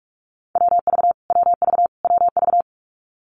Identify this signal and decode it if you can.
Morse code, heard as a keyed tone, about 44 wpm, 700 Hz. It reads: WV WV WV
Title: US States Repeated 3x 45wpm